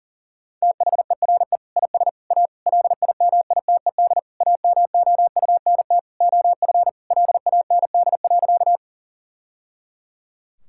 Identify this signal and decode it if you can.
Morse code, heard as a keyed tone, about 37 wpm, 700 Hz. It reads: THERE IS A LIMITED AMOUNT OF LAND.
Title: Sentences from Top 1000 Words - Review Part 6 - 40wpm